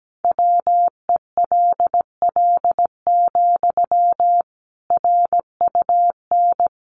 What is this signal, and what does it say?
Morse code, heard as a keyed tone, about 17 wpm, 700 Hz. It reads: WELL, RUN
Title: Sentences from Top 300 Words 17wpm